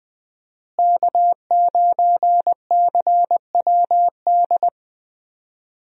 Morse code, heard as a keyed tone, about 20 wpm, 700 Hz. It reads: K9CWD
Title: USA Call Signs - 5 Characters - Repeated 2x - 20wpm